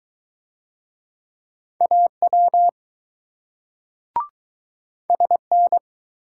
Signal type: Morse code

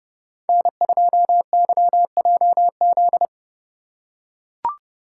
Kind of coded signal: Morse code

